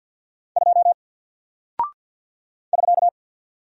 Morse code, heard as a keyed tone, about 48 wpm, 700 Hz. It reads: 2 <SK>
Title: Single Character (Letter:Number:Punctuation:Prosign) Alt 50wpm